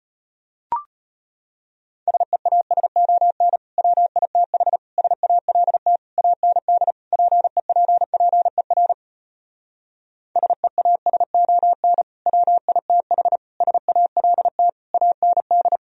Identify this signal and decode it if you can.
Morse code, heard as a keyed tone, about 33 wpm, 700 Hz. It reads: SEASON WITH SALT AND PEPPER SEASON WITH SALT AND